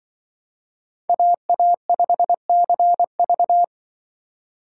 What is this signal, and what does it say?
Morse code, heard as a keyed tone, about 23 wpm, 700 Hz. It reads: AA5CV
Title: Speed Racing - USA 2x2 Call Signs 20wpm